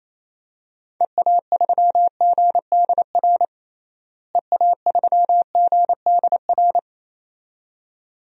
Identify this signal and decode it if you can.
Morse code, heard as a keyed tone, about 28 wpm, 700 Hz. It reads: EA3GDR EA3GDR
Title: DXCC-100 Call Signs - 6 Characters - Repeated 2x - 28wpm